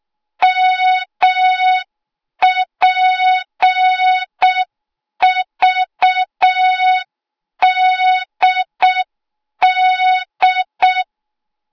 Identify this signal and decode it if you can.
Morse code, heard as a keyed tone, about 6 wpm, 3700 Hz. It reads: MPVDD
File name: Morze-HA2MN / hang / 08KXC HA2MN